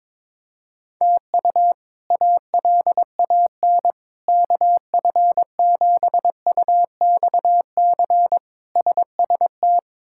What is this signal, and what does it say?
Morse code, heard as a keyed tone, about 22 wpm, 700 Hz. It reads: TU ALAN KF7UXC SST
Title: K1USN SST QSO Practice - 22wpm